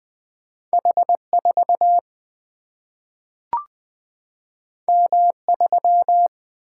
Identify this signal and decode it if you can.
Morse code, heard as a keyed tone, about 20 wpm, 700 Hz. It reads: H4 M3
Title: Single-Letter Single-Number 20wpm